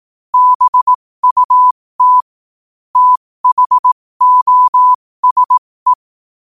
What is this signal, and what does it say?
Morse code, heard as a keyed tone, about 18 wpm, spaced as for 15 wpm, 1000 Hz. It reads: BUT THOSE
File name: qotdfiles / 15wpm